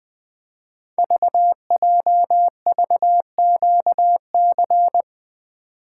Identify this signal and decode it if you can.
Morse code, heard as a keyed tone, about 21 wpm, 700 Hz. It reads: VJVQC
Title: Code Groups - 5 Letters - 20wpm